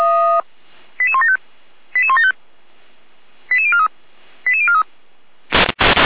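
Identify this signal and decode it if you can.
Five-tone selective call (ZVEI), heard as quick runs of five tones, tones of about 70 ms. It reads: 88175 88175 88043 88043
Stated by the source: ZVEI five-tone selective call received by SDR.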